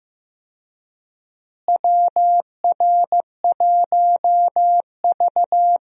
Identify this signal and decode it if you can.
Morse code, heard as a keyed tone, about 15 wpm, 700 Hz. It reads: WR1V
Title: USA Call Signs - 4 Characters - Repeated 2x - 15wpm